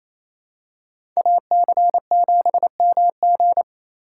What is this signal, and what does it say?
Morse code, heard as a keyed tone, about 28 wpm, 700 Hz. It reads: AC7MG
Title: USA Call Signs - 5 Characters - 4x Character Spacing - 28wpm